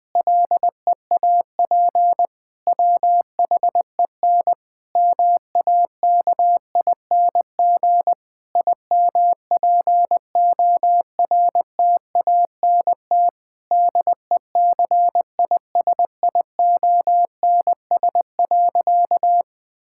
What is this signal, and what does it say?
Morse code, heard as a keyed tone, about 21 wpm, 700 Hz. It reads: LEAP WHEN MAKING IMPORTANT DECISIONS.
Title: Difficult Sentences with Hints - 20wpm